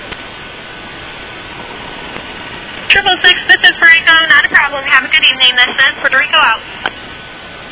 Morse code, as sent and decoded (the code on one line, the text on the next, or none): none